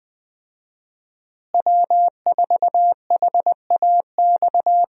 Morse code, heard as a keyed tone, about 21 wpm, 700 Hz. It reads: W4HAX
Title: USA Call Signs - 5 Characters - Repeated 3x - 20wpm